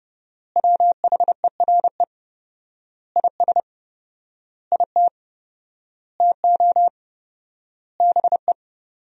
Morse code, heard as a keyed tone, about 30 wpm, 700 Hz. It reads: WHERE IS IT TO BE
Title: 4x Word-Spacing - Sentences from Top 300 Words - 30wpm